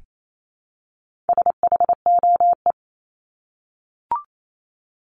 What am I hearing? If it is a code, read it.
Morse code, heard as a keyed tone, about 28 wpm, 700 Hz. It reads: SHOE
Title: Top 1751-2000 Words 28wpm